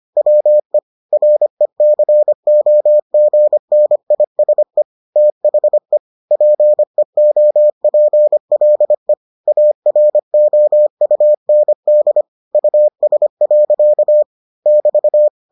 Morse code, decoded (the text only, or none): WE RECOGNISE THE PEOPLE AROUND US. <BT>